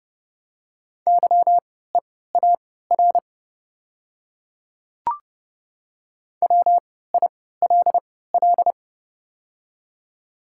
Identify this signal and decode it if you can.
Morse code, heard as a keyed tone, about 30 wpm, spaced as for 17 wpm, 690 Hz. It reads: YEAR WILL
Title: 3x Character Spacing - Top 200 Words - 30wpm